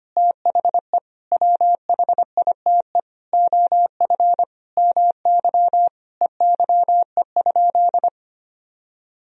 Morse code, heard as a keyed tone, about 25 wpm, 700 Hz. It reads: THE WHITE OF MY EYE?
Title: Sentences from Top 500 Words and Abbreviations 25wpm